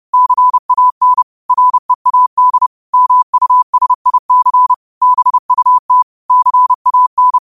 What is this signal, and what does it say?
Morse code, heard as a keyed tone, about 30 wpm, 1000 Hz. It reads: CAN READ MUSIC BUT CAN
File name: qotdfiles / 30wpm